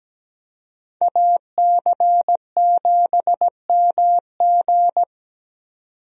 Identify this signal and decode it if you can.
Morse code, heard as a keyed tone, about 17 wpm, 700 Hz. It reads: AC7MG
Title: USA Call Signs - 5 Characters - 17wpm